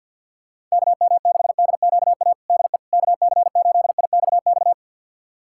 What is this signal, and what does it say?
Morse code, heard as a keyed tone, about 48 wpm, 700 Hz. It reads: KM6DQA DE KC8IXX
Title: USA Callsign DE Callsign 50wpm